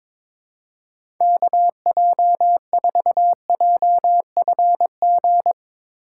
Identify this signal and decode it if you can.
Morse code, heard as a keyed tone, about 22 wpm, 700 Hz. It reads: KJ4JFG
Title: USA Call Signs - 6 Characters - 8x Character Spacing - 22wpm